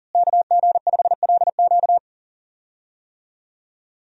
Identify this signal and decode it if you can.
Morse code, heard as a keyed tone, about 40 wpm, 700 Hz. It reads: KG5LQ
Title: USA Call Signs - 5 Characters - Repeated 3x - 40wpm